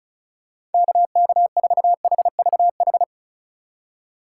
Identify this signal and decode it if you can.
Morse code, heard as a keyed tone, about 34 wpm, 700 Hz. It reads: KK4HVH
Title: USA Call Signs - 4x Character Spacing - 35wpm